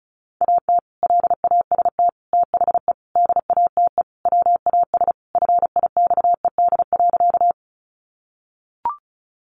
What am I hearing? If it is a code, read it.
Morse code, heard as a keyed tone, about 34 wpm, 700 Hz. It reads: AT LAST THE DATE WAS FIXED.
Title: Sentences from Top 1100 Words 35wpm